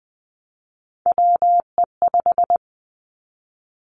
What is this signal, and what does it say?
Morse code, heard as a keyed tone, about 20 wpm, 700 Hz. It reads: WE5
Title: Three Challenging Letter-Number Combinations 20wpm